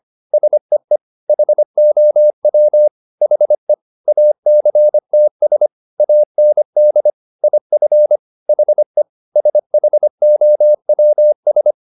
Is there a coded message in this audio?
Yes: Morse code from 0.3 to 11.7 s, heard as a keyed tone, about 25 wpm, 600 Hz. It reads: SEE HOW HE ACTS AND IF HE SHOWS